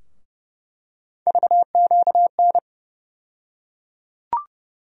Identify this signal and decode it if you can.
Morse code, heard as a keyed tone, about 30 wpm, 700 Hz. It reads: VQN
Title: USA Call Sign Postfixes - 3 Letters 30wpm